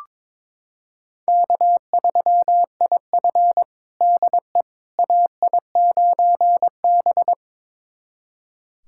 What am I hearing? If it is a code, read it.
Morse code, heard as a keyed tone, about 22 wpm, 700 Hz. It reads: K3IF DE AI9B
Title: USA Callsign DE Callsign - 4 Characters - 22wpm